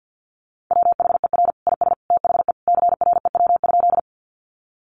Morse code, heard as a keyed tone, about 50 wpm, 700 Hz. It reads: WHER IS THE CREW?
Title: Sentences from Top 1500 Words 50wpm